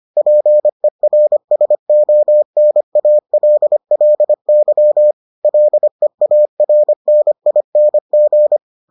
Morse code, heard as a keyed tone, about 25 wpm, 600 Hz. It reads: PERSONALLY LEARNING